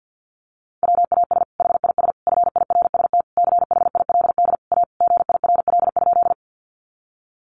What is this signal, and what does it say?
Morse code, heard as a keyed tone, about 50 wpm, 700 Hz. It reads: WAS HIS FIRST CHILD A GIRL?
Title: Sentences from Top 800 Words 50wpm